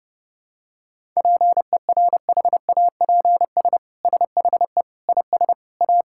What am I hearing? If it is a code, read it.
Morse code, heard as a keyed tone, about 32 wpm, 700 Hz. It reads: PERHAPS SHE IS A
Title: Sentences from Top 1000 Words - Review Part 5 - 30wpm